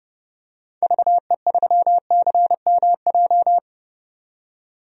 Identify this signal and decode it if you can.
Morse code, heard as a keyed tone, about 32 wpm, 700 Hz. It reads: VE3CMJ
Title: DXCC-100 Call Signs - 6 Characters - 30wpm